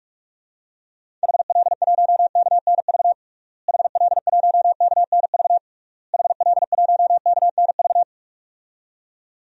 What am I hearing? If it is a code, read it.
Morse code, heard as a keyed tone, about 46 wpm, 700 Hz. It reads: HL1KNV HL1KNV HL1KNV
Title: DXCC-100 Call Signs - 6 Characters - Repeated 3x - 45wpm